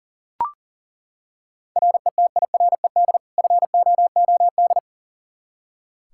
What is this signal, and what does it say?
Morse code, heard as a keyed tone, about 38 wpm, 700 Hz. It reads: RETIRED FOOD
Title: K1USN SST QSO Practice 40wpm